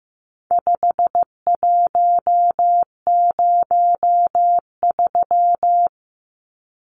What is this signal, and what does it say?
Morse code, heard as a keyed tone, about 15 wpm, 700 Hz. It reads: 5103
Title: Numbers - 4-Digit 15wpm